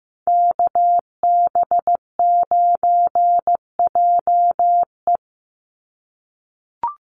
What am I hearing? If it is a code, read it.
Morse code, heard as a keyed tone, about 15 wpm, 700 Hz. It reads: KB9JE
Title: USA Call Signs 15wpm